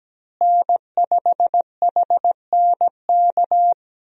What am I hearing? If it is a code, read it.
Morse code, heard as a keyed tone, about 17 wpm, 700 Hz. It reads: N5HNK
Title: USA Call Signs - 5 Characters - Repeated 2x - 17wpm